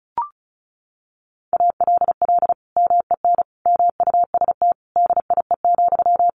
Morse code, heard as a keyed tone, about 34 wpm, 700 Hz. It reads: ALL MEN MUST DIE,
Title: Sentences from Top 1300 Words 35wpm